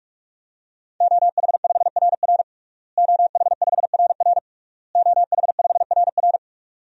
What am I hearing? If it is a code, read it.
Morse code, heard as a keyed tone, about 47 wpm, 700 Hz. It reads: OH5RR OH5RR OH5RR
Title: DXCC-100 Call Signs - 5 Characters - Repeated 3x - 45wpm